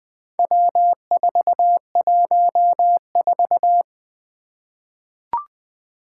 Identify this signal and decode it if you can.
Morse code, heard as a keyed tone, about 20 wpm, 700 Hz. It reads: W414
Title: Four Challenging Letter-Number Combinations 20wpm